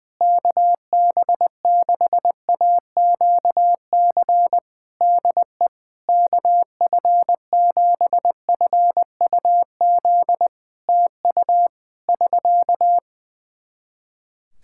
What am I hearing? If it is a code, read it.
Morse code, heard as a keyed tone, about 20 wpm, 700 Hz. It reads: KB6AQC DE KF7FUZ TU <SK>
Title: Practice QSO - Exchange 4 - Ending - 20wpm